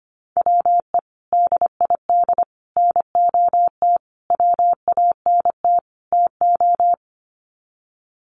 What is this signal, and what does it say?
Morse code, heard as a keyed tone, about 25 wpm, 690 Hz. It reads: WE DID NOT WANT TO
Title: Sets of 5 Words Encore 25wpm